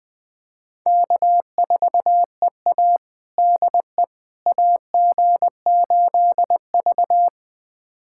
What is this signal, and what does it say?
Morse code, heard as a keyed tone, about 20 wpm, 700 Hz. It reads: K4EA DE AG8V
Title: USA Callsign DE Callsign - 4 Characters - 20wpm